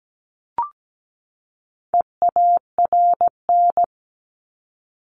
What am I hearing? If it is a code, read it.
Morse code, heard as a keyed tone, about 17 wpm, 700 Hz. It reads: EARN
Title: Top 1251-1500 Words 17wpm